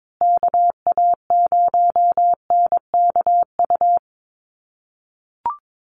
Morse code, heard as a keyed tone, about 22 wpm, 700 Hz. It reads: KA0NKU
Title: USA Call Signs Repeated 3x 22wpm